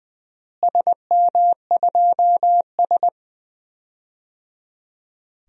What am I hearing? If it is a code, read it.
Morse code, heard as a keyed tone, about 21 wpm, 700 Hz. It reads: SM2S
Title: DXCC-100 Call Signs - 4 Characters - Repeated 3x - 20wpm